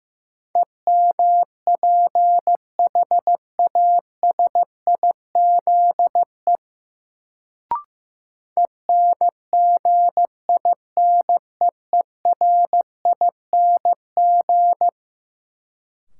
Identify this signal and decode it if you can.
Morse code, heard as a keyed tone, about 15 wpm, 700 Hz. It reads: EMPHASIZE ENGINEERING
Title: Sentences from Top 1500 Words - New Word Drill Alt - 15wpm